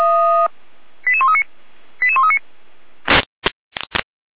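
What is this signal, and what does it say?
Five-tone selective call (ZVEI), heard as quick runs of five tones, tones of about 80 ms. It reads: 88139 88139
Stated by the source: Received on an SDR; ZVEI five-tone selective call.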